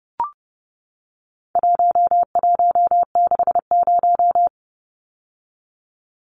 Morse code, heard as a keyed tone, about 30 wpm, 700 Hz. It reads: 1160
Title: Numbers - 4-Digit 30wpm